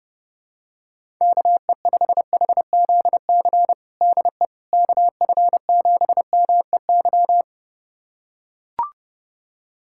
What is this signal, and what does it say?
Morse code, heard as a keyed tone, about 32 wpm, 700 Hz. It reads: KE5HZC DE KF7MEY
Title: USA Callsign DE Callsign - 6 Characters - 30wpm